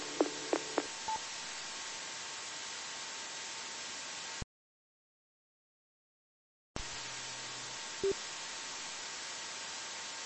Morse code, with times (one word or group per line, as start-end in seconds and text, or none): none